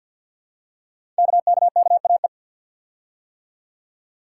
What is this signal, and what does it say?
Morse code, heard as a keyed tone, about 50 wpm, 700 Hz. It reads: KKKAE
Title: Code Groups - 5 Letters - 50wpm